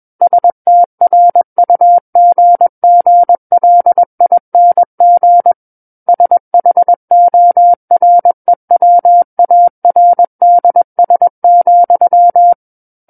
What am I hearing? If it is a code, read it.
Morse code, heard as a keyed tone, about 21 wpm, 700 Hz. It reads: STRUGGLING SHOREWARDS,
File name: Book One / 12-Chapter 12 - 21 WPM